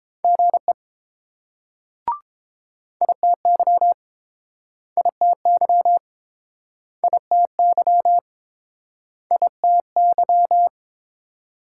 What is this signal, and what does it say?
Morse code, heard as a keyed tone, about 30 wpm, 700 Hz. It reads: GE ITY ITY ITY ITY